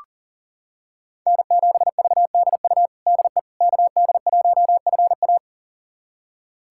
Morse code, heard as a keyed tone, about 40 wpm, 700 Hz. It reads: N7VDU DE KD1FA
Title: USA Callsign DE Callsign - 5 Characters - 40wpm